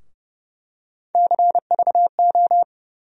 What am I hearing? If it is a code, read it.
Morse code, heard as a keyed tone, about 30 wpm, 700 Hz. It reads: CVO
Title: USA Call Sign Postfixes - 3 Letters 30wpm